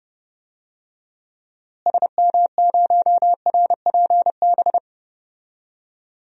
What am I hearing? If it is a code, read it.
Morse code, heard as a keyed tone, about 30 wpm, 700 Hz. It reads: SM0RPB
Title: DXCC-100 Call Signs - 6 Characters - Repeated 3x - 30wpm